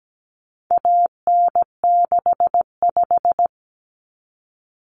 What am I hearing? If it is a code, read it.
Morse code, heard as a keyed tone, about 17 wpm, 700 Hz. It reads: AN65
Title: Common Cut Numbers - 4-Digit 17wpm